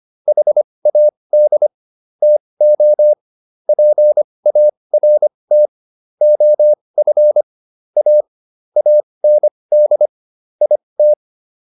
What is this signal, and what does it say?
Morse code, heard as a keyed tone, about 25 wpm, spaced as for 20 wpm, 600 Hz. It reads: HAD TO PART OF A AND IT